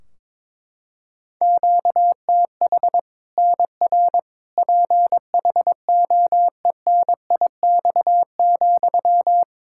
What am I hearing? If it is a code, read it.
Morse code, heard as a keyed tone, about 22 wpm, 700 Hz. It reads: QTH NR PHOENIX,
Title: Practice QSO - Exchange 1 - Beginning - 22wpm